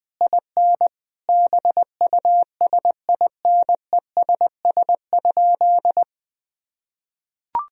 Morse code, heard as a keyed tone, about 20 wpm, 700 Hz. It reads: IN BUSINESS?
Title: Sentences from Top 1000 Words - Review Part 3 - 20wpm